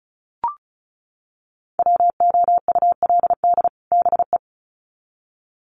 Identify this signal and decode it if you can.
Morse code, heard as a keyed tone, about 34 wpm, 700 Hz. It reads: WOULD BE
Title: Sets of 2 Words 35wpm